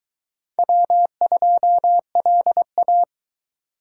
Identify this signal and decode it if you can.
Morse code, heard as a keyed tone, about 22 wpm, 700 Hz. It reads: W2LA